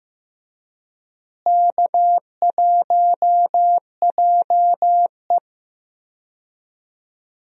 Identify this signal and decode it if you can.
Morse code, heard as a keyed tone, about 15 wpm, 700 Hz. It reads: K1JE
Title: USA Call Signs - 4 Characters - Repeated 3x - 15wpm